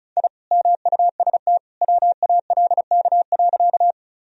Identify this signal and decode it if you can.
Morse code, heard as a keyed tone, about 34 wpm, 700 Hz. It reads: I MUST WALK.
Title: Sentences from Top 1000 Words - Review Part 1 - 35wpm